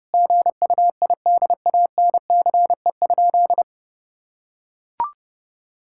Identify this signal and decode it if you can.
Morse code, heard as a keyed tone, about 30 wpm, 700 Hz. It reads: GUIDANCE?
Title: Sentences from Top 1600 Words 30wpm